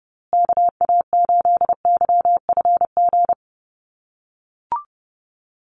Morse code, heard as a keyed tone, about 30 wpm, 700 Hz. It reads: KA8YFG
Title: USA Call Signs Repeated 2x 30wpm